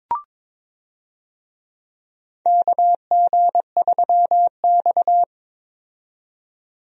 Morse code, heard as a keyed tone, about 22 wpm, 700 Hz. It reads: KG3X
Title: USA Call Signs - 4 Characters - 22wpm